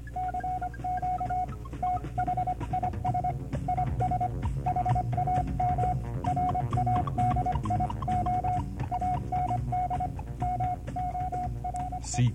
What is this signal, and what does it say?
Morse code, heard as a keyed tone, about 25 wpm, 730 Hz, with noise. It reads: CQ THIS IS HAM RADIO AND MORE